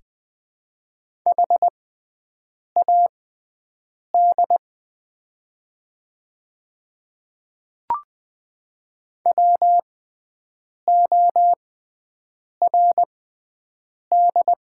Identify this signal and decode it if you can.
Morse code, heard as a keyed tone, about 20 wpm, spaced as for 7 wpm, 700 Hz. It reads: HAD WORD
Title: 6x Character Spacing - Top 100 Words - 20wpm